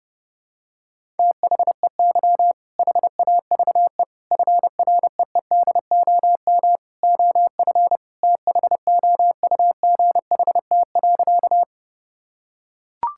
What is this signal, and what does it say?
Morse code, heard as a keyed tone, about 30 wpm, 700 Hz. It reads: THEY HAVE FREEDOM OF THOUGHT.
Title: Sentences from Top 1000 Words - Review Part 5 - 30wpm